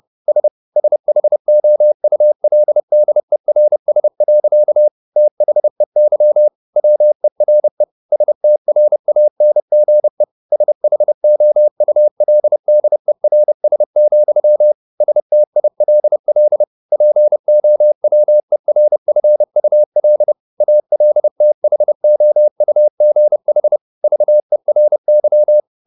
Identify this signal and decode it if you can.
Morse code, heard as a keyed tone, about 30 wpm, 600 Hz. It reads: S SHOULDERS. THEY WERE STRANGE SHOULDERS, STILL POWERFUL ALTHOUGH VERY
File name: the old man and the sea / the old man and the sea 1 30wpm